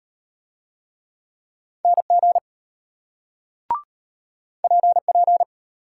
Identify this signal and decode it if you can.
Morse code, heard as a keyed tone, about 33 wpm, 700 Hz. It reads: NG PP